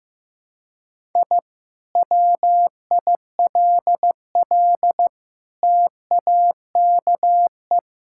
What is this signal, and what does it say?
Morse code, heard as a keyed tone, about 15 wpm, 700 Hz. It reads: I WILL TAKE
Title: Sentences from Top 1000 Words - Review Part 6 - 15wpm